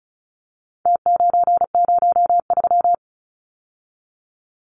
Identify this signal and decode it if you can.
Morse code, heard as a keyed tone, about 34 wpm, 700 Hz. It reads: T903
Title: Numbers - Contest Style - 4-Digit 35wpm